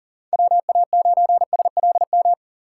Morse code, heard as a keyed tone, about 40 wpm, 700 Hz. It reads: WA9SLM
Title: USA Call Signs - 3x Character Spacing - 40wpm